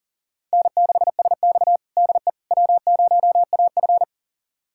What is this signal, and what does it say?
Morse code, heard as a keyed tone, about 40 wpm, 700 Hz. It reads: N6SX DE W0AF
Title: USA Callsign DE Callsign - 4 Characters - 40wpm